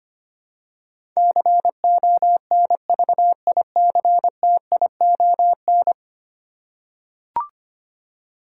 Morse code, heard as a keyed tone, about 25 wpm, 700 Hz. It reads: CONVICTION
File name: Ten Letter Words - 25wpm